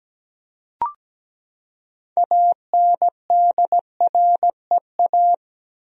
Morse code, heard as a keyed tone, about 17 wpm, 700 Hz. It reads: ANDREA
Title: USA Common Names - 17wpm